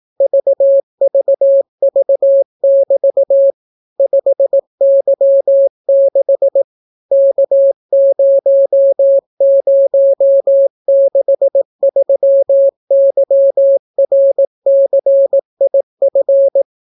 Morse code, heard as a keyed tone, about 18 wpm, 550 Hz. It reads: VVV<BT> 5Y6 K0063YRCIF